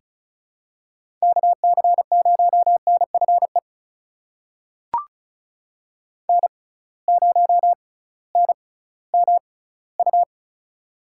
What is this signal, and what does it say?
Morse code, heard as a keyed tone, about 34 wpm, spaced as for 19 wpm, 700 Hz. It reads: KC0NFE N0NMU
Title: USA Call Signs - 6x Character Spacing - 35wpm